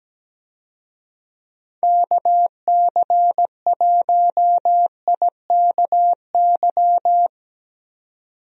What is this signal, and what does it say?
Morse code, heard as a keyed tone, about 17 wpm, 700 Hz. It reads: KC1IKY
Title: USA Call Signs - 6 Characters - 17wpm